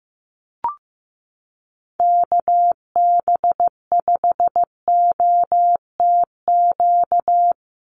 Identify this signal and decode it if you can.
Morse code, heard as a keyed tone, about 15 wpm, 700 Hz. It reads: KB5OTQ
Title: USA Call Signs 15wpm